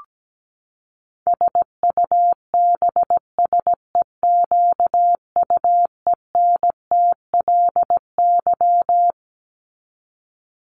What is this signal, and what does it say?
Morse code, heard as a keyed tone, about 17 wpm, 700 Hz. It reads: SUBSEQUENTLY
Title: LY Suffix - Top 100 Words with Base Words 17wpm